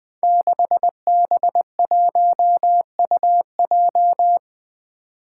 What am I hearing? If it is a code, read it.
Morse code, heard as a keyed tone, about 20 wpm, 700 Hz. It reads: 6B1UJ
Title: Five Challenging Letter-Number Combinations 20wpm